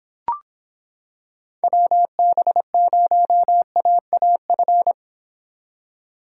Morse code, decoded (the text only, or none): WB0AAF